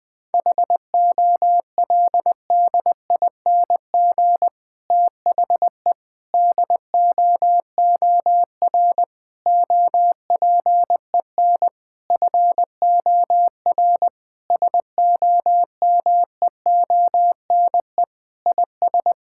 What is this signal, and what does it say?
Morse code, heard as a keyed tone, about 21 wpm, 700 Hz. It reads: HOLDING THE DOOR OPEN FOR SOMEONE IS
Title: Difficult Sentences with Hints - 20wpm